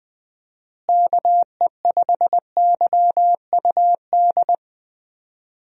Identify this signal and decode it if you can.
Morse code, heard as a keyed tone, about 21 wpm, 700 Hz. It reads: KE5YUD
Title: USA Call Signs - 6 Characters - Repeated 2x - 20wpm